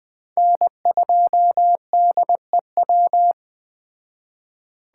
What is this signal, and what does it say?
Morse code, heard as a keyed tone, about 21 wpm, 700 Hz. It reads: N2DEW
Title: Speed Racing - USA Call Signs 17wpm